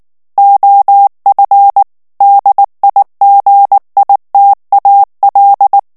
Morse code, heard as a keyed tone, about 19 wpm, 800 Hz. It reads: OF DIGITAL
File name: ANGLAIS texte clair 3 vitesse 18 mots minute